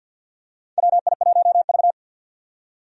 Morse code, heard as a keyed tone, about 53 wpm, 700 Hz. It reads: WI1V
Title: USA Call Signs - 4 Characters - 50wpm